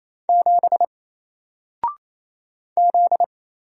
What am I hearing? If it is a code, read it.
Morse code, heard as a keyed tone, about 27 wpm, 700 Hz. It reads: 7 Z
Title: Single Character (Letter:Number:Punctuation:Prosign) Alt 28wpm